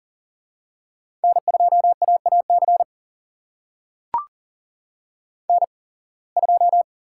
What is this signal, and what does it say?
Morse code, heard as a keyed tone, about 40 wpm, spaced as for 26 wpm, 700 Hz. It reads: N2AAC N2
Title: USA Call Signs - 5 Characters - 8x Character Spacing - 40wpm